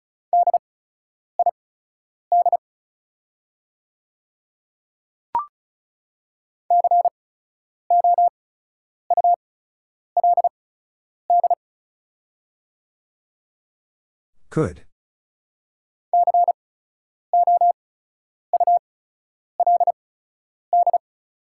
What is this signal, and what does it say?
Morse code, heard as a keyed tone, about 34 wpm, spaced as for 10 wpm, 690 Hz. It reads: DID COULD COULD
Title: 8x Character Spacing - Top 200 Words - 35wpm